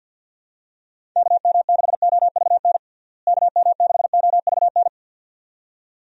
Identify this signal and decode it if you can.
Morse code, heard as a keyed tone, about 49 wpm, 700 Hz. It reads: KM6OVN KM6OVN
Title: USA Call Signs - 6 Characters - Repeated 2x - 50wpm